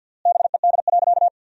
Morse code, heard as a keyed tone, about 47 wpm, 700 Hz. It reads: BED.
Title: 3x Word-Spacing - Sentences from Top 500 Words - 50wpm